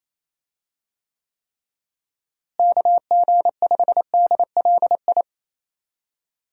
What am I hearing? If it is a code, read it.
Morse code, heard as a keyed tone, about 28 wpm, 700 Hz. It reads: KG5DLI
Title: USA Call Signs - 6 Characters - Repeated 3x - 28wpm